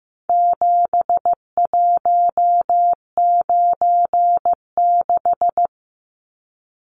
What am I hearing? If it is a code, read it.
Morse code, heard as a keyed tone, about 15 wpm, 700 Hz. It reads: 7196
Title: Numbers - 4-Digit 15wpm